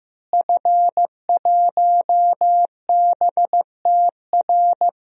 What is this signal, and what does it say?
Morse code, heard as a keyed tone, about 15 wpm, 700 Hz. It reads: F1BTR
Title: DXCC-100 Call Signs - 5 Characters - Repeated 3x - 15wpm